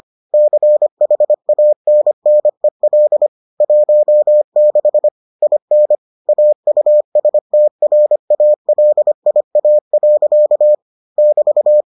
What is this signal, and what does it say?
Morse code, heard as a keyed tone, about 25 wpm, 600 Hz. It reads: CHANNEL 16 IN AUSTRALIA. <BT>